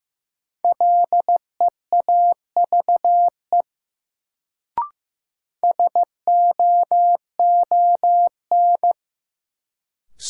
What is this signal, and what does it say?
Morse code, heard as a keyed tone, about 15 wpm, 700 Hz. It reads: LEAVE SOON
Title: Sentences from Top 400 Words - New Word Drill - 15wpm